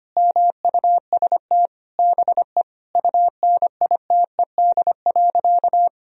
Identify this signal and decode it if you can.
Morse code, heard as a keyed tone, about 25 wpm, 700 Hz. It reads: MUST BE UNITED.
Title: Sentences from Top 1000 Words - Review Part 4 - 25wpm